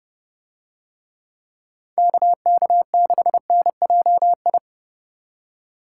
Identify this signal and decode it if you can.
Morse code, heard as a keyed tone, about 29 wpm, 700 Hz. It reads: KK6NJI